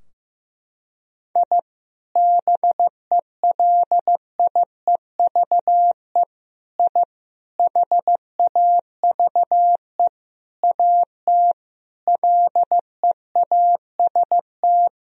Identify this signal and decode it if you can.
Morse code, heard as a keyed tone, about 15 wpm, 700 Hz. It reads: I BELIEVE I HAVE AT LEAST
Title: Sentences from Top 1000 Words - Review Part 2 - 15wpm